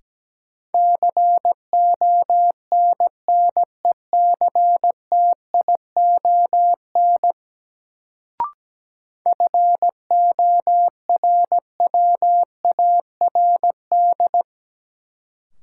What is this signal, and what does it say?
Morse code, heard as a keyed tone, about 17 wpm, 700 Hz. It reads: CONNECTION FORWARD
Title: Sentences from Top 1300 Words - New Word Drill Alt - 17wpm